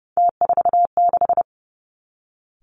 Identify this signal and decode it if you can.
Morse code, heard as a keyed tone, about 30 wpm, 700 Hz. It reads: T46
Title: Three Challenging Letter-Number Combinations 30wpm